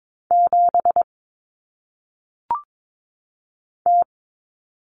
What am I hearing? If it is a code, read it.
Morse code, heard as a keyed tone, about 22 wpm, 700 Hz. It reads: 7 T
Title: Common Cut Numbers - 1-Digit 22wpm